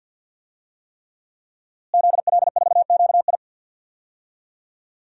Morse code, heard as a keyed tone, about 48 wpm, 700 Hz. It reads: ZL4QI
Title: DXCC-100 Call Signs - 5 Characters - Repeated 3x - 50wpm